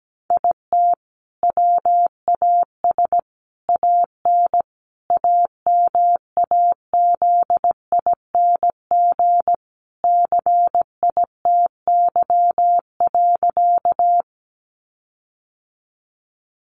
Morse code, heard as a keyed tone, about 17 wpm, 700 Hz. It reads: IT WAS AN AMAZING CITY.
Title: Sentences from Top 1500 Words 17wpm